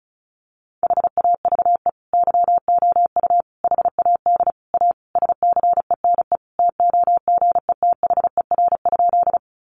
Morse code, heard as a keyed tone, about 34 wpm, 700 Hz. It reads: HAVE YOU HAD A SCENE TOGETHER?
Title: Sentences from Top 1100 Words 35wpm